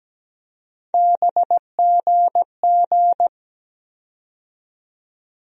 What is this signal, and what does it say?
Morse code, heard as a keyed tone, about 17 wpm, 700 Hz. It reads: BGG
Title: USA Call Sign Postfixes - 3 Letters 17wpm